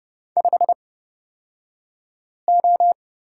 Morse code, heard as a keyed tone, about 31 wpm, spaced as for 5 wpm, 700 Hz. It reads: 5O
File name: Lesson 36d - BT - Two Characters 30wpm